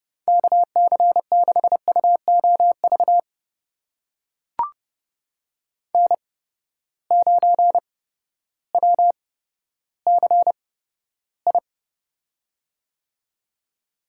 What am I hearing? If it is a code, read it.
Morse code, heard as a keyed tone, about 29 wpm, spaced as for 14 wpm, 700 Hz. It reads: KC6UOV N9WCI
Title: USA Call Signs - 8x Character Spacing - 30wpm